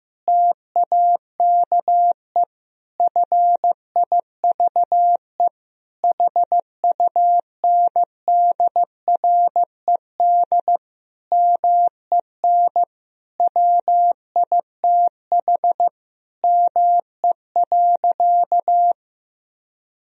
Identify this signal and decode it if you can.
Morse code, heard as a keyed tone, about 15 wpm, 700 Hz. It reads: TAKE FIVE HUNDRED MEN WITH ME.
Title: Sentences from Top 1000 Words - Review Part 2 - 15wpm